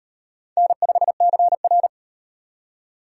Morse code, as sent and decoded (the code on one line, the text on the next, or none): -. ..... -.-. .-.
N5CR